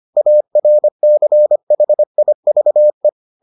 Morse code, heard as a keyed tone, about 25 wpm, 600 Hz. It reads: ARCHIVE